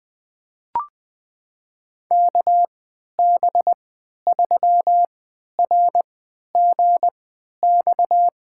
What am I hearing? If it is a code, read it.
Morse code, heard as a keyed tone, about 20 wpm, spaced as for 11 wpm, 700 Hz. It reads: KB3RGX
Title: USA Call Signs - 6 Characters - 3x Character Spacing - 20wpm